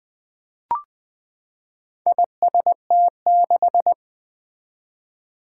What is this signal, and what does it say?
Morse code, heard as a keyed tone, about 20 wpm, 700 Hz. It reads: IST6
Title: Four Challenging Letter-Number Combinations 20wpm